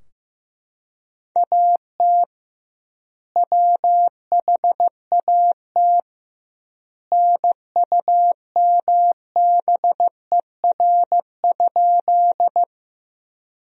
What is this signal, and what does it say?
Morse code, heard as a keyed tone, about 15 wpm, 700 Hz. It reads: AT WHAT NUMBER?
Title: 2x Word-Spacing - Sentences from Top 100 Words - 15wpm